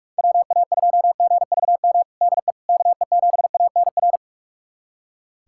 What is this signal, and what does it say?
Morse code, heard as a keyed tone, about 43 wpm, 700 Hz. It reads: WA2GVM DE KE7ANR
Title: USA Callsign DE Callsign - 6 Characters - 45wpm